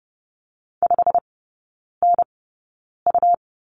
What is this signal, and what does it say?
Morse code, heard as a keyed tone, about 30 wpm, spaced as for 9 wpm, 700 Hz. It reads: 5NU
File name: Lesson 17e - Letter u - Three Characters 30wpm